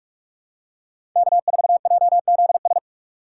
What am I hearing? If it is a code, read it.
Morse code, heard as a keyed tone, about 46 wpm, 700 Hz. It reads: K4JZS
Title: USA Call Signs - 5 Characters - 3x Character Spacing - 45wpm